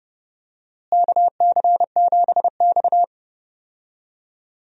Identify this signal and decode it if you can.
Morse code, heard as a keyed tone, about 30 wpm, 700 Hz. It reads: KC7X
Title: USA Call Signs - 4 Characters - Repeated 3x - 30wpm